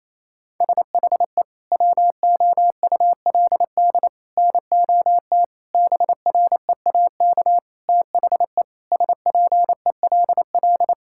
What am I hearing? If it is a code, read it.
Morse code, heard as a keyed tone, about 28 wpm, 700 Hz. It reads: SHE WOULD NOT BREAK THE SPELL
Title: Sentences from Top 1000 Words - Review Part 6 - 28wpm